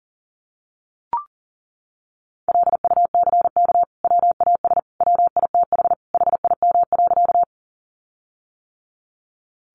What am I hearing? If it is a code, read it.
Morse code, heard as a keyed tone, about 40 wpm, 700 Hz. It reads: LUCK WAS WITH HIM.
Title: Sentences from Top 1500 Words 40wpm